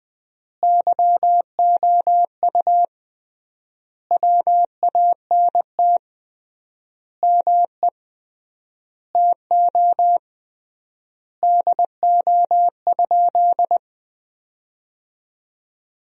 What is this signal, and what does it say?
Morse code, heard as a keyed tone, about 20 wpm, 700 Hz. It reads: YOU WANT ME TO DO?
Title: 3x Word-Spacing - Sentences from Top 200 Words - 20wpm